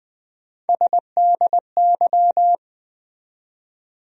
Morse code, heard as a keyed tone, about 20 wpm, 700 Hz. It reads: SDY
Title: USA Call Sign Postfixes - 3 Letters 20wpm